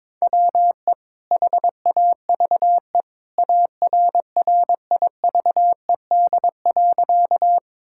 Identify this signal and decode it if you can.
Morse code, heard as a keyed tone, about 21 wpm, 700 Hz. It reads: WE HAVE ARRIVED.
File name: practice / 22wpm / Sentences with 2-Word Phrases - 22wpm